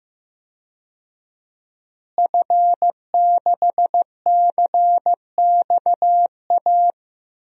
Morse code, heard as a keyed tone, about 15 wpm, 700 Hz. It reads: F6CXA
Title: DXCC-100 Call Signs - 5 Characters - Repeated 3x - 15wpm